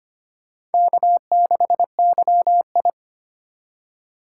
Morse code, heard as a keyed tone, about 25 wpm, 700 Hz. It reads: K6YI
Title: USA Call Signs - 2x Character Spacing - 25wpm